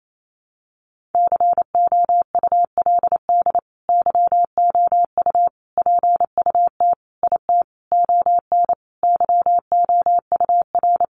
Morse code, heard as a keyed tone, about 28 wpm, 700 Hz. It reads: COULD YOU PUT IT ON YOUR
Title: Sentences from Top 1400 Words 28wpm